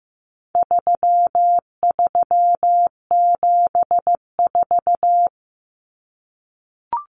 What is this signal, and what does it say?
Morse code, heard as a keyed tone, about 15 wpm, 700 Hz. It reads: 3374
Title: Numbers - 4-Digit 15wpm